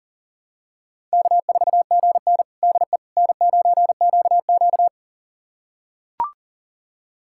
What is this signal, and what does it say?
Morse code, heard as a keyed tone, about 40 wpm, 700 Hz. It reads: K4GN DE N9QQ
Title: USA Callsign DE Callsign - 4 Characters - 40wpm